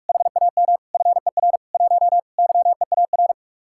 Morse code, heard as a keyed tone, about 44 wpm, 700 Hz. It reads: HAM FER 1 YEAR